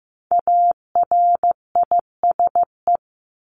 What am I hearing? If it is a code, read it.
Morse code, heard as a keyed tone, about 15 wpm, 700 Hz. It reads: ARISE
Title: Top 1501-1750 Words 15wpm